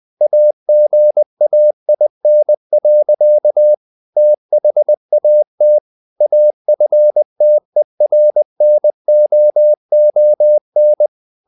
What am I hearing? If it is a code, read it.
Morse code, heard as a keyed tone, about 20 wpm, 600 Hz. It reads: AGAIN. THAT AFTERNOON
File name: the old man and the sea part 2 20wpm